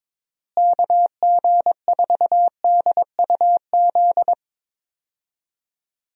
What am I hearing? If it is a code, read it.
Morse code, heard as a keyed tone, about 22 wpm, 700 Hz. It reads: KG4DUZ
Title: USA Call Signs - 6 Characters - Repeated 3x - 22wpm